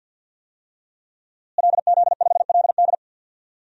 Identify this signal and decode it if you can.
Morse code, heard as a keyed tone, about 51 wpm, 700 Hz. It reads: LZ5LD
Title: DXCC-100 Call Signs - 5 Characters - Repeated 2x - 50wpm